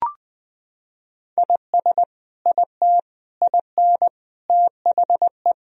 Morse code, heard as a keyed tone, about 20 wpm, 700 Hz. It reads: IS IT IN THE
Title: Sentences from Top 1000 Words - Review Part 4 - 20wpm